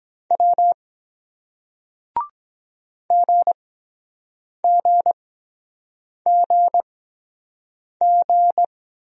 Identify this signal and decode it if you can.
Morse code, heard as a keyed tone, about 22 wpm, spaced as for 7 wpm, 700 Hz. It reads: W GGGG